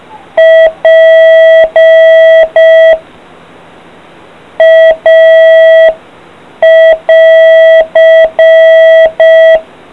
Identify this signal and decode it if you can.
Morse code, heard as a keyed tone, about 5 wpm, 650 Hz. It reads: P A <AR>